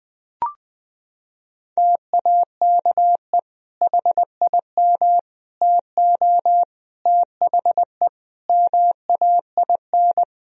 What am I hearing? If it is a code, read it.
Morse code, heard as a keyed tone, about 21 wpm, 700 Hz. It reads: TAKE HIM TO THE MAIN
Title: Sentences from Top 1000 Words - Review Part 4 - 20wpm